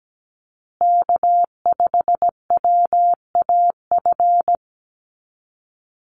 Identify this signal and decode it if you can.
Morse code, heard as a keyed tone, about 17 wpm, 700 Hz. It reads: K5WAF
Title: Call Signs Repeated 2x - 17wpm